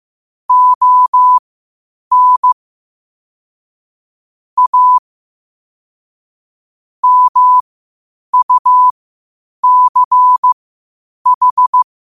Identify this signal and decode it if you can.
Morse code, heard as a keyed tone, about 15 wpm, spaced as for 8 wpm, 1000 Hz. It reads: ON A MUCH
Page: Quotes for Thu, 14 Aug 2025 in Morse Code at 8 words per minute.